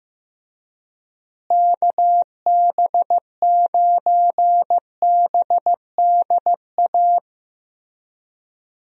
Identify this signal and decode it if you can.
Morse code, heard as a keyed tone, about 16 wpm, 700 Hz. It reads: KB9BDA
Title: USA Call Signs - 6 Characters - Repeated 2x - 15wpm